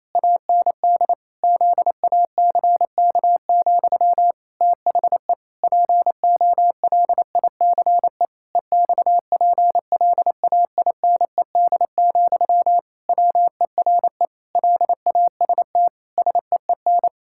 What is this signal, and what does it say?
Morse code, heard as a keyed tone, about 27 wpm, 700 Hz. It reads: AND ZACK, THE POLICE EXPLAINED, WERE LAST SEEN
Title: Pangram Challenge v2 - 28wpm